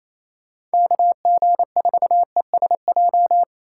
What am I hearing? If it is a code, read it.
Morse code, heard as a keyed tone, about 33 wpm, 700 Hz. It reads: KG4ESJ
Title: Speed Racing - USA Call Signs 28wpm